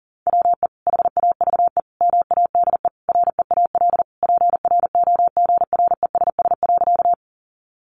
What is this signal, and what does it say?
Morse code, heard as a keyed tone, about 40 wpm, 700 Hz. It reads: WE HAVE MADE REAL PROGRESS.
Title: Sentences from Top 1400 Words 40wpm